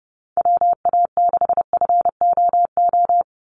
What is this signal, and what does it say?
Morse code, heard as a keyed tone, about 30 wpm, 700 Hz. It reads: WA6FOO
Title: USA Call Signs 30wpm